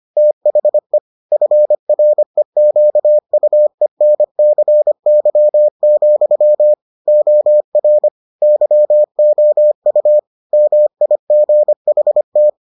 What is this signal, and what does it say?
Morse code, heard as a keyed tone, about 25 wpm, 600 Hz. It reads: THE FREQUENCY, OR YOU MIGHT